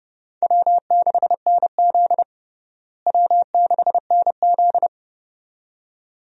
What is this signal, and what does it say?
Morse code, heard as a keyed tone, about 29 wpm, 700 Hz. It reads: W6NZ W6NZ
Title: USA Call Signs - 4 Characters - Repeated 2x - 30wpm